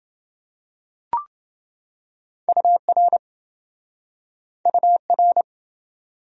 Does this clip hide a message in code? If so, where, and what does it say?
Yes: Morse code from 2.5 to 5.4 s, heard as a keyed tone, about 26 wpm, 700 Hz. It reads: UR UR